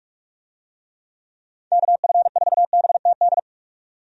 Morse code, heard as a keyed tone, about 46 wpm, 700 Hz. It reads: KF4BTD
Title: USA Call Signs - 6 Characters - 45wpm